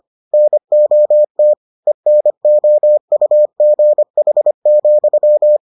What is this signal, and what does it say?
Morse code, heard as a keyed tone, about 25 wpm, 600 Hz. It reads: NOT ENOUGH,